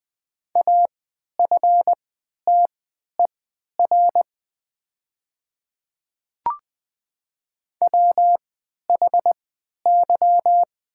Morse code, heard as a keyed tone, about 20 wpm, spaced as for 11 wpm, 700 Hz. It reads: AFTER WHY
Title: 3x Character Spacing - Top 200 Words - 20wpm